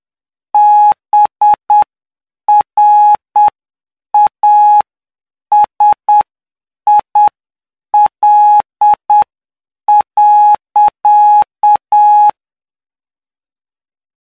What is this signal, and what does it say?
Morse code, heard as a keyed tone, about 8 wpm, spaced as for 7 wpm, 840 Hz. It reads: BRASIL.
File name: cw / py2lcd / 14-PY2LCD-Li��o 23